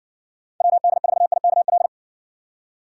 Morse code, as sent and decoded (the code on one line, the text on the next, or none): .-- -.. ....- .. -.- .-..
WD4IKL